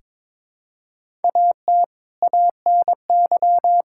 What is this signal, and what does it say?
Morse code, heard as a keyed tone, about 22 wpm, 700 Hz. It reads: AT ANY
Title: Sentences from Top 1000 Words - Review Part 5 - 22wpm